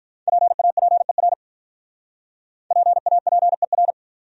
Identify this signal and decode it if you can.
Morse code, heard as a keyed tone, about 46 wpm, 700 Hz. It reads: PAPER PAPER